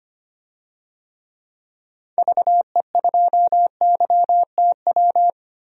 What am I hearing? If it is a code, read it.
Morse code, heard as a keyed tone, about 25 wpm, 700 Hz. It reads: VE2YTW
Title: DXCC-100 Call Signs - 6 Characters - 25wpm